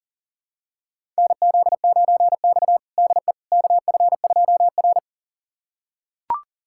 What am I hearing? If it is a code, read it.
Morse code, heard as a keyed tone, about 37 wpm, 700 Hz. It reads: NZ9X DE KF2R
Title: USA Callsign DE Callsign - 4 Characters - 40wpm